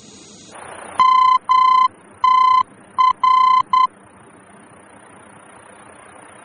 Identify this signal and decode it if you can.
Morse code, heard as a keyed tone, about 10 wpm, 1020 Hz. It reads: MTR
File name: germany / mx / MTR-D-VOR